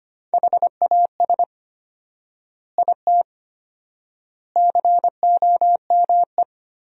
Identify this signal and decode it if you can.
Morse code, heard as a keyed tone, about 25 wpm, 700 Hz. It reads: HAS IT COME
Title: 4x Word-Spacing - Sentences from Top 100 Words - 25wpm